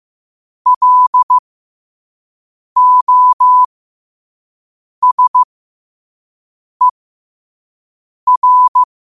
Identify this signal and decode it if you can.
Morse code, heard as a keyed tone, about 15 wpm, spaced as for 5 wpm, 1000 Hz. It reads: LOSER
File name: qotdfiles / 5wpm